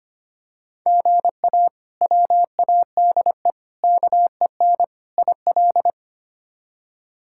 Morse code, heard as a keyed tone, about 25 wpm, 700 Hz. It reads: GA WADE KEN IL
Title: K1USN SST QSO Practice - 25wpm